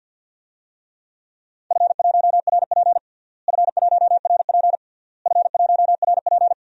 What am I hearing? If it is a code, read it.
Morse code, heard as a keyed tone, about 52 wpm, 700 Hz. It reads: F1RP F1RP F1RP
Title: DXCC-100 Call Signs - 4 Characters - Repeated 3x - 50wpm